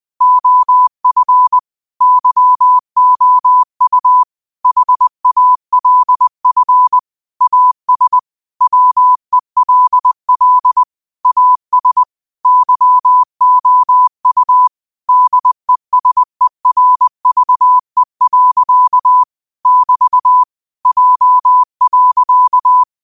Morse code, heard as a keyed tone, about 20 wpm, 1000 Hz. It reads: OF YOU HALF AS WELL AS YOU DESERVE. <BT> J.
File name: qotdfiles / 20wpm